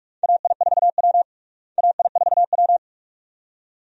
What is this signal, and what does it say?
Morse code, heard as a keyed tone, about 45 wpm, 700 Hz. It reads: AI4W AI4W
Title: USA Call Signs - 4 Characters - Repeated 2x - 45wpm